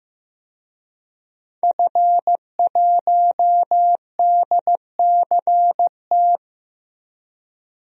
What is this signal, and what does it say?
Morse code, heard as a keyed tone, about 15 wpm, 700 Hz. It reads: F1DCT
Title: DXCC-100 Call Signs - 5 Characters - 15wpm